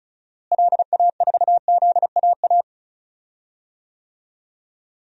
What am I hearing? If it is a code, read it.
Morse code, heard as a keyed tone, about 36 wpm, 700 Hz. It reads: LA4ZAA
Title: DXCC-100 Call Signs - 6 Characters - Repeated 2x - 35wpm